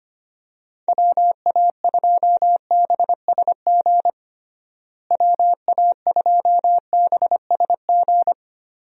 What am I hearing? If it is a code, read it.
Morse code, heard as a keyed tone, about 25 wpm, 700 Hz. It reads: WA2BSG WA2BSG
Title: USA Call Signs - 6 Characters - Repeated 2x - 25wpm